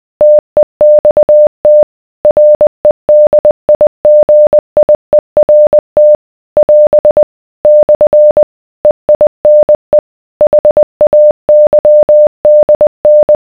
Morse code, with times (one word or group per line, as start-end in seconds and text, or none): 0.2-1.8 TEXT
2.2-6.2 REDIGIERT
6.6-7.2 <AS>
7.6-8.4 /
8.8-10.0 EINE
10.4-13.4 HAYDN